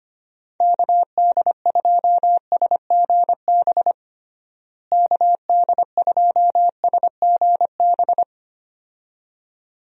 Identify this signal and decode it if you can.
Morse code, heard as a keyed tone, about 25 wpm, 700 Hz. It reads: KD2SGB KD2SGB
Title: USA Call Signs - 6 Characters - Repeated 2x - 25wpm